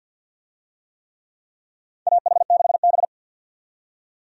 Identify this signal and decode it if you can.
Morse code, heard as a keyed tone, about 50 wpm, 700 Hz. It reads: AH6B
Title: USA Call Signs - 4 Characters - 50wpm